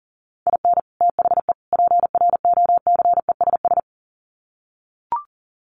Morse code, heard as a keyed tone, about 40 wpm, 700 Hz. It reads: IN THE PROCESS
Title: Sets of 3 Words 40wpm